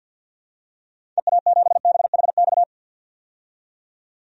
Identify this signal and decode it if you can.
Morse code, heard as a keyed tone, about 47 wpm, 700 Hz. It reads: EA7BHX
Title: DXCC-100 Call Signs - 6 Characters - 50wpm